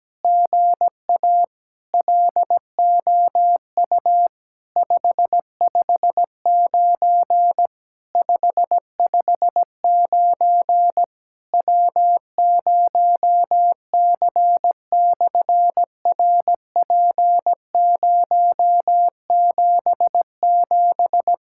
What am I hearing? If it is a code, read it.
Morse code, heard as a keyed tone, about 17 wpm, 700 Hz. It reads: GA LOU 559 559 W0C/RP077
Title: SOTA QSO Practice 17wpm